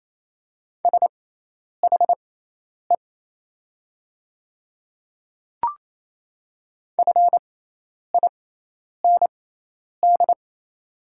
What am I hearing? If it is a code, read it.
Morse code, heard as a keyed tone, about 27 wpm, spaced as for 10 wpm, 700 Hz. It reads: SHE FIND
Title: 6x Character Spacing - Top 100 Words - 28wpm